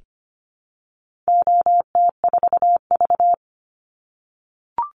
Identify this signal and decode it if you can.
Morse code, heard as a keyed tone, about 25 wpm, 700 Hz. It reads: OT4V
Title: Call Signs - 25wpm